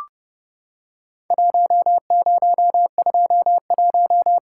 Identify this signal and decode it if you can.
Morse code, heard as a keyed tone, about 30 wpm, 700 Hz. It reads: 1021
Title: Worked All Europe DX Contest - QTC Exchange - 30wpm